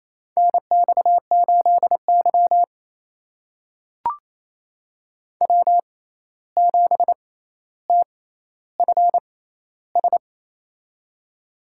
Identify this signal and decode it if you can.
Morse code, heard as a keyed tone, about 28 wpm, spaced as for 13 wpm, 700 Hz. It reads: NX8Y W7TFS
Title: USA Call Signs - 6x Character Spacing - 28wpm